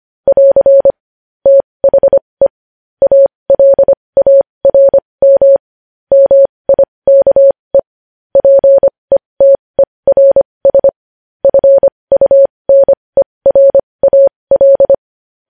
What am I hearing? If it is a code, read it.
Morse code, heard as a keyed tone, about 25 wpm, spaced as for 20 wpm, 550 Hz. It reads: <AR> THE ALARM MIKE PETERS FUNERAL
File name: podcast / News headlines-1x-25x20